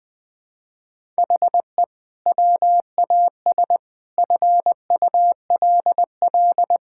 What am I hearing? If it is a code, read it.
Morse code, heard as a keyed tone, about 20 wpm, 700 Hz. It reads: HE WAS FULL
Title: Sentences from Top 1000 Words - Review Part 6 - 20wpm